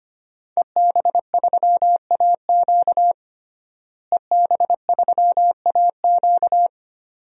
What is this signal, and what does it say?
Morse code, heard as a keyed tone, about 25 wpm, 700 Hz. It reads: EB3AQ EB3AQ
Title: DXCC-100 Call Signs - 5 Characters - Repeated 2x - 25wpm